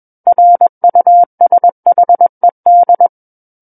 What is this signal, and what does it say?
Morse code, heard as a keyed tone, about 21 wpm, 700 Hz. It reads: RUSHED
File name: Book One / 12-Chapter 12 - 21 WPM